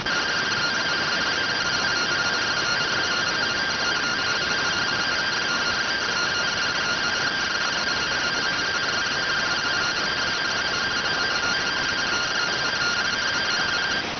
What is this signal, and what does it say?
Two-tone teleprinter signal, about 170 Hz shift.